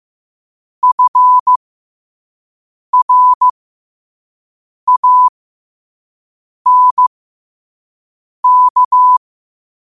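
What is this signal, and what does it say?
Morse code, heard as a keyed tone, about 15 wpm, spaced as for 5 wpm, 1000 Hz. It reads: FRANK
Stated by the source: Morse code quote of the day sent at 5 wpm.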